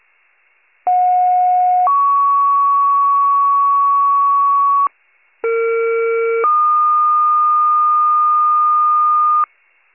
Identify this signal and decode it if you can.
Two-tone page, heard as a one-second tone, then a three-second tone, about 730 then 1120 Hz, about 460 then 1220 Hz. It is heard from 0.9 to 9.4 s.
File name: pager / 21 RESCUE / wcalert